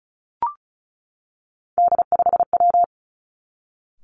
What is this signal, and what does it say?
Morse code, heard as a keyed tone, about 36 wpm, 700 Hz. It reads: D5W
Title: Three Challenging Letter-Number Combinations 35wpm